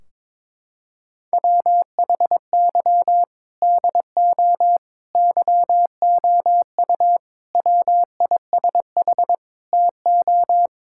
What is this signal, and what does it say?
Morse code, heard as a keyed tone, about 21 wpm, 700 Hz. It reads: WHY DO YOU WISH TO
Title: Sentences from Top 1600 Words 22wpm